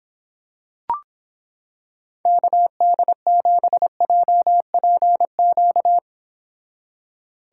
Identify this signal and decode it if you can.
Morse code, heard as a keyed tone, about 26 wpm, 700 Hz. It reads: KD7JPQ